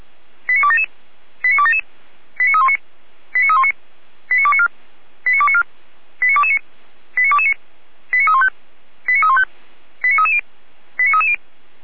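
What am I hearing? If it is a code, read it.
Five-tone selective call (ZVEI), heard as quick runs of five tones, tones of about 70 ms. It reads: 89277 89277 89319 89319 89294 89294 89229 89229 89315 89315 89330 89330
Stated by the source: ZVEI five-tone call 89125 received by SDR.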